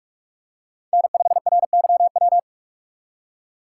Morse code, heard as a keyed tone, about 45 wpm, 700 Hz. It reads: N5RYW
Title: USA Call Signs - 2x Character Spacing - 45wpm